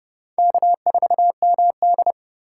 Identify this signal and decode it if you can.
Morse code, heard as a keyed tone, about 32 wpm, 700 Hz. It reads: K4MD
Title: USA Call Signs - 4 Characters - 2x Character Spacing - 30wpm